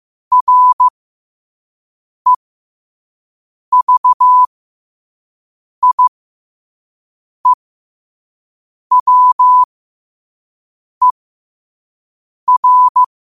Morse code, heard as a keyed tone, about 15 wpm, spaced as for 5 wpm, 1000 Hz. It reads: REVIEWER
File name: qotdfiles / 5wpm